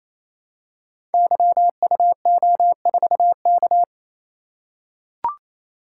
Morse code, heard as a keyed tone, about 29 wpm, 700 Hz. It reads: YUO4K
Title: Code Groups - 5 Letter:Numbers - 28wpm